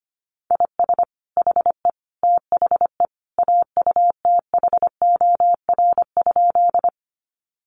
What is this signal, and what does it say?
Morse code, heard as a keyed tone, about 25 wpm, 700 Hz. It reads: IS HE THE AUTHOR?
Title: Sentences from Top 1200 Words 25wpm